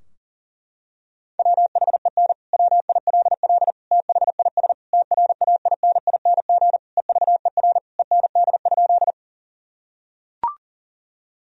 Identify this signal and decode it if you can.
Morse code, heard as a keyed tone, about 40 wpm, 700 Hz. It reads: WHEN WILL THIS TRAINING EVER END?
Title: Sentences from Top 1000 Words - Review Part 4 - 40wpm